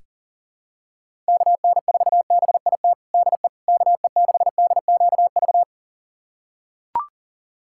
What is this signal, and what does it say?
Morse code, heard as a keyed tone, about 40 wpm, 700 Hz. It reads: KN4BIT DE KE6DQV
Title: USA Callsign DE Callsign - 6 Characters - 40wpm